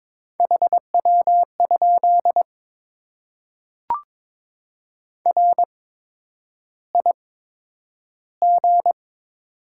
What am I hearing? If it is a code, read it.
Morse code, heard as a keyed tone, about 22 wpm, spaced as for 9 wpm, 700 Hz. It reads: HW? RIG
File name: Top 20 QSO Elements - 8x Character Spacing - 22wpm